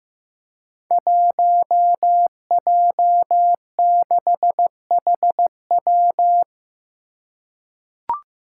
Morse code, heard as a keyed tone, about 15 wpm, 700 Hz. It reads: 1J6HW
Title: Five Challenging Letter-Number Combinations 15wpm